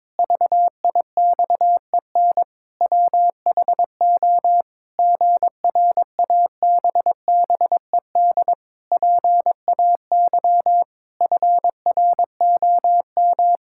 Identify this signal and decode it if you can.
Morse code, heard as a keyed tone, about 22 wpm, 700 Hz. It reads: VIXEN WHO GRABBED PAY FROM
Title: Pangram Challenge v2 - 22wpm